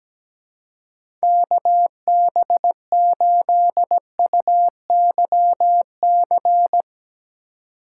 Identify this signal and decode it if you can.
Morse code, heard as a keyed tone, about 17 wpm, 700 Hz. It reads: KB8UYC
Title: USA Call Signs - 6 Characters - 3x Character Spacing - 17wpm